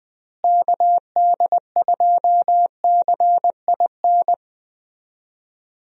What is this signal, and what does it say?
Morse code, heard as a keyed tone, about 20 wpm, 700 Hz. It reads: KD2CIN
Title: USA Call Signs - 6 Characters - Repeated 3x - 20wpm